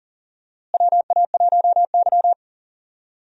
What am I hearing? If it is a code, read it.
Morse code, heard as a keyed tone, about 40 wpm, 700 Hz. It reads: WA1Y
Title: USA Call Signs - 4 Characters - Repeated 2x - 40wpm